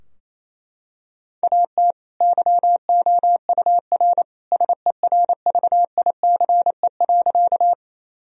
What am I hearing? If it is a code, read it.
Morse code, heard as a keyed tone, about 28 wpm, 700 Hz. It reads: AT YOUR SERVICE.
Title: Sentences from Top 1000 Words - Review Part 4 - 28wpm